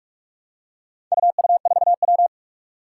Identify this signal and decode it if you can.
Morse code, heard as a keyed tone, about 44 wpm, 700 Hz. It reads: UU4W
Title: Four Challenging Letter-Number Combinations 45wpm